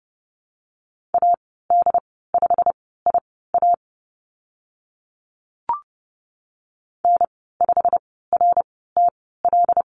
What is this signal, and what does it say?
Morse code, heard as a keyed tone, about 30 wpm, spaced as for 17 wpm, 700 Hz. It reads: AD5IA N5RTL
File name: Lesson 15g - Number 5 - Call Signs 30wpm